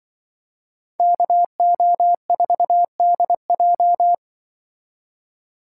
Morse code, heard as a keyed tone, about 24 wpm, 700 Hz. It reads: KO4DJ
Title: Speed Racing - USA 2x2 Call Signs 20wpm